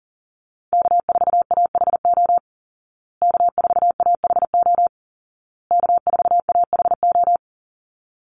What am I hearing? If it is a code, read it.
Morse code, heard as a keyed tone, about 40 wpm, 700 Hz. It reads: K4AHO K4AHO K4AHO
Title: Call Signs Repeated 2x - 40wpm